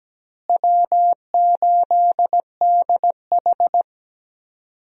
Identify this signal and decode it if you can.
Morse code, heard as a keyed tone, about 17 wpm, 700 Hz. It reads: W8DH
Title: USA Call Signs - 4 Characters - Repeated 2x - 17wpm